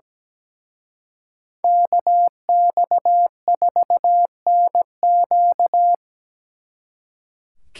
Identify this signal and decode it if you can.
Morse code, heard as a keyed tone, about 20 wpm, 700 Hz. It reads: KX4NQ
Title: Speed Racing - USA 2x2 Call Signs 17wpm